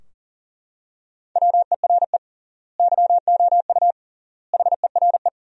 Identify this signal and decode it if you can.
Morse code, heard as a keyed tone, about 40 wpm, 700 Hz. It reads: WERE YOU HERE
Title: 3x Word-Spacing - Sentences from Top 200 Words - 40wpm